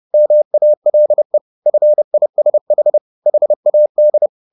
Morse code, heard as a keyed tone, about 30 wpm, 600 Hz. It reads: MALE FISH HAD
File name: the old man and the sea / the old man and the sea 1 30wpm